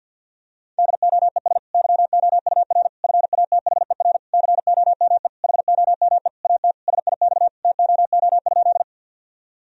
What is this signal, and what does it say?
Morse code, heard as a keyed tone, about 48 wpm, 700 Hz. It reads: DOES YOUR FATHER COME HOME AT SIX TOO?
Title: Sentences from Top 1000 Words - Review Part 2 - 50wpm